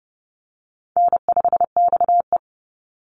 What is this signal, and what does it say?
Morse code, heard as a keyed tone, about 30 wpm, 700 Hz. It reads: N5XE
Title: Easy Call Signs - 30wpm